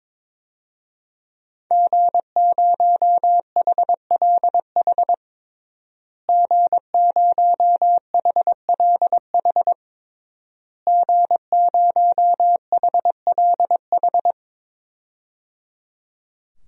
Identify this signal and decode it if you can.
Morse code, heard as a keyed tone, about 22 wpm, 700 Hz. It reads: G0HLH G0HLH G0HLH
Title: DXCC-100 Call Signs - 5 Characters - Repeated 3x - 22wpm